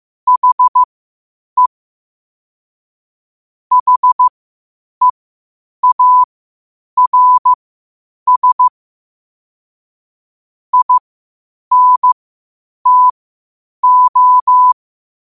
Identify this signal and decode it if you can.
Morse code, heard as a keyed tone, about 15 wpm, spaced as for 8 wpm, 1000 Hz. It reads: HE HEARS INTO
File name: qotdfiles / 8wpm